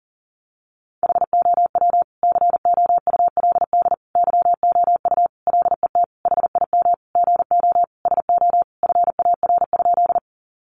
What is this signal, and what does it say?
Morse code, heard as a keyed tone, about 40 wpm, 700 Hz. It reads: HOW COULD YOU LET HIM GO SO FAR?
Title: Sentences from Top 300 Words 40wpm